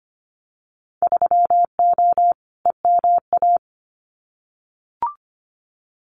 Morse code, heard as a keyed tone, about 25 wpm, 690 Hz. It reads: 3O EMA
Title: Winter Field Day Exchanges 25wpm